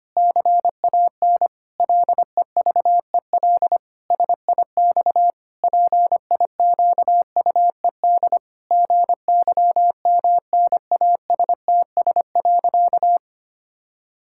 Morse code, heard as a keyed tone, about 25 wpm, 700 Hz. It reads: CAN LEVEL SIX PIQUED GYMNASTS.
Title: Pangram Challenge v2 - 25wpm